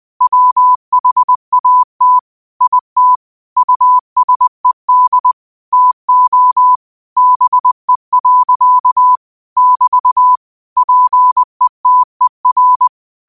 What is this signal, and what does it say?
Morse code, heard as a keyed tone, about 20 wpm, 1000 Hz. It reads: WHAT IT USED TO BE. <BT> PETER